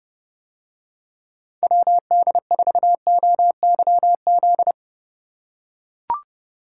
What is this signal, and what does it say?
Morse code, heard as a keyed tone, about 30 wpm, 700 Hz. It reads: WD4OYZ